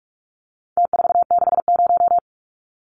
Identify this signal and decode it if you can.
Morse code, heard as a keyed tone, about 44 wpm, 700 Hz. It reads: T460
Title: Numbers - Contest Style - 4-Digit 45wpm